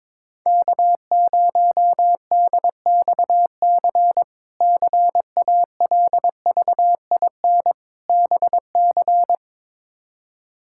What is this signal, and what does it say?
Morse code, heard as a keyed tone, about 22 wpm, 700 Hz. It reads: K0DXC CALVIN BC
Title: K1USN SST QSO Practice - 22wpm